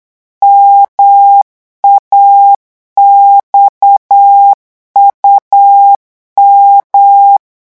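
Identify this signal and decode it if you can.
Morse code, heard as a keyed tone, about 8 wpm, 800 Hz. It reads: MAXUM
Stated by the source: Morse code revision lesson at 8 words per minute, from a French amateur radio course.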